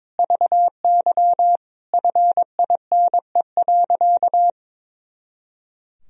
Morse code, heard as a keyed tone, about 22 wpm, 700 Hz. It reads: VY FINE.
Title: Sentences from Top 500 Words and Abbreviations 22wpm